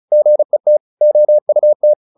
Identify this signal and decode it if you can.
Morse code, heard as a keyed tone, about 34 wpm, 600 Hz. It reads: GET OUT